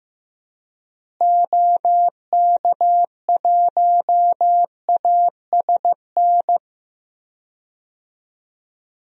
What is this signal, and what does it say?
Morse code, heard as a keyed tone, about 15 wpm, 700 Hz. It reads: OK1ASN
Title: DXCC-100 Call Signs - 6 Characters - Repeated 3x - 15wpm